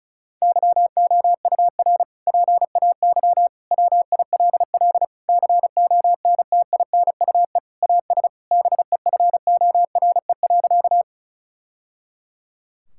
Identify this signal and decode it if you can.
Morse code, heard as a keyed tone, about 36 wpm, 700 Hz. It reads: YOUR PAY WILL CONTINUE AS BEFORE.
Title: Sentences from Top 1000 Words - Review Part 6 - 35wpm